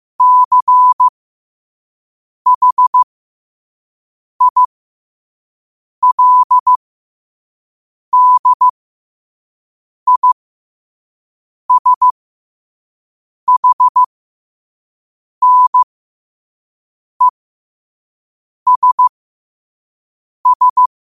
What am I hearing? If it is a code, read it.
Morse code, heard as a keyed tone, about 15 wpm, spaced as for 5 wpm, 1000 Hz. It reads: CHILDISHNESS
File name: qotdfiles / 5wpm